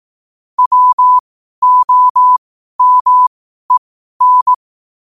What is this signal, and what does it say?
Morse code, heard as a keyed tone, about 18 wpm, spaced as for 12 wpm, 1000 Hz. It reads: WOMEN
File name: qotdfiles / 12wpm